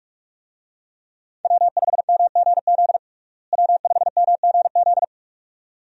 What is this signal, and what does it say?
Morse code, heard as a keyed tone, about 46 wpm, 700 Hz. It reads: W5MGZ W5MGZ
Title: USA Call Signs - 5 Characters - Repeated 2x - 45wpm